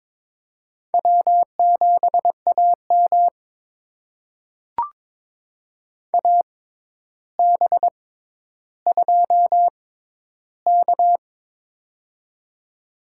Morse code, heard as a keyed tone, about 22 wpm, spaced as for 11 wpm, 700 Hz. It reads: W7AM AB2K
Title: USA Call Signs - 4 Characters - 6x Character Spacing - 22wpm